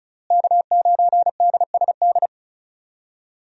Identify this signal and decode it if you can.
Morse code, heard as a keyed tone, about 34 wpm, 700 Hz. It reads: K9DSD
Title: USA Call Signs - 5 Characters - Repeated 3x - 35wpm